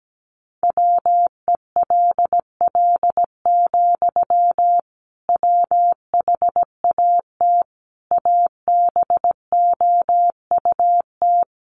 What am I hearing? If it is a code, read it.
Morse code, heard as a keyed tone, about 17 wpm, 700 Hz. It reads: WELL, WHAT ABOUT
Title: Sentences from Top 1200 Words 17wpm